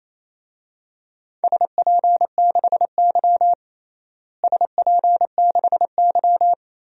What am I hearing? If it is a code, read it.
Morse code, heard as a keyed tone, about 28 wpm, 700 Hz. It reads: SP6Y SP6Y
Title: DXCC-100 Call Signs - 4 Characters - Repeated 2x - 28wpm